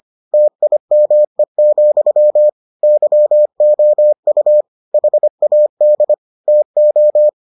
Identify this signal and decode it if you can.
Morse code, heard as a keyed tone, about 25 wpm, 600 Hz. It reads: TIME, YOU HAD TO